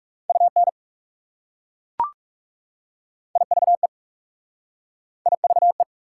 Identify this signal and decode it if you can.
Morse code, heard as a keyed tone, about 38 wpm, 690 Hz. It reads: UN IVE IVE